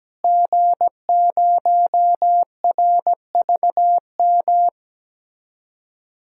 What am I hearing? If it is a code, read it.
Morse code, heard as a keyed tone, about 17 wpm, 700 Hz. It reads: G0RVM
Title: DXCC-100 Call Signs - 5 Characters - Repeated 3x - 17wpm